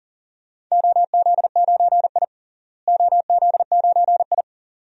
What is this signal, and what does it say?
Morse code, heard as a keyed tone, about 40 wpm, 700 Hz. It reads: OZ9I OZ9I
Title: DXCC-100 Call Signs - 4 Characters - Repeated 2x - 40wpm